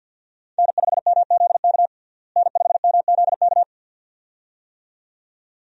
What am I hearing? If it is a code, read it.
Morse code, heard as a keyed tone, about 52 wpm, 700 Hz. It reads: N5MZK N5MZK
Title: USA Call Signs - 5 Characters - Repeated 2x - 50wpm